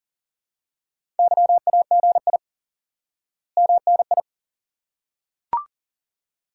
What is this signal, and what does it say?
Morse code, heard as a keyed tone, about 40 wpm, 700 Hz. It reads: YAGI MNI
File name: course / instant-qso-element-course-3 / IQE Course 3 - Lesson 89 - YAGI - Two Elements - 40wpm